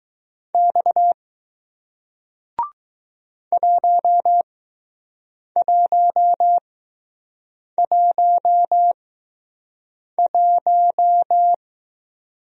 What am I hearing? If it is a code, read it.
Morse code, heard as a keyed tone, about 20 wpm, spaced as for 6 wpm, 700 Hz. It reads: X 1111